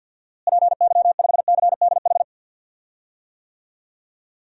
Morse code, heard as a keyed tone, about 52 wpm, 700 Hz. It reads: PY5CDH
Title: DXCC-100 Call Signs - 6 Characters - Repeated 3x - 50wpm